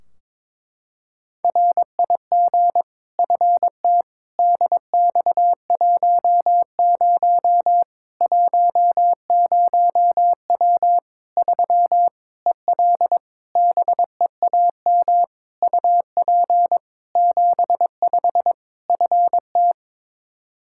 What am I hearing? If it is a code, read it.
Morse code, heard as a keyed tone, about 22 wpm, 700 Hz. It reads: RIG FT DX10 10W 3 EL BEAM UP 75 FT
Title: K1USN SST QSO Practice 22wpm